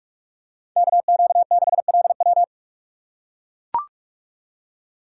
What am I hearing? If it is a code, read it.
Morse code, heard as a keyed tone, about 41 wpm, 700 Hz. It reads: KQ6LW